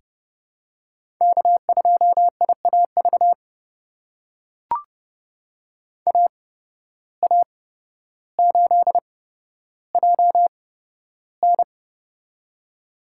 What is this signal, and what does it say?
Morse code, heard as a keyed tone, about 32 wpm, spaced as for 13 wpm, 700 Hz. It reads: K2IAV AA8JN
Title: USA Call Signs - 5 Characters - 8x Character Spacing - 30wpm